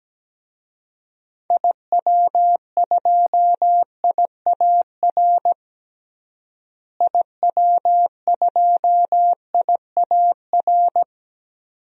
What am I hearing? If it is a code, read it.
Morse code, heard as a keyed tone, about 17 wpm, 700 Hz. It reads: IW2IAR IW2IAR
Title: DXCC-100 Call Signs - 6 Characters - Repeated 2x - 17wpm